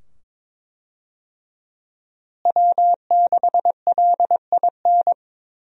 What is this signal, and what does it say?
Morse code, heard as a keyed tone, about 21 wpm, 700 Hz. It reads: W6LIN
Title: USA Call Signs - 5 Characters - Repeated 2x - 22wpm